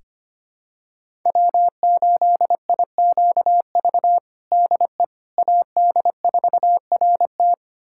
Morse code, heard as a keyed tone, about 25 wpm, 700 Hz. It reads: W8IQV DE AD4RT
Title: USA Callsign DE Callsign 25wpm